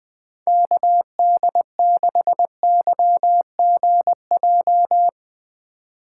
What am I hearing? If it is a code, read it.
Morse code, heard as a keyed tone, about 21 wpm, 700 Hz. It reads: KD6YGJ
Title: USA Call Signs - 6 Characters - 2x Character Spacing - 20wpm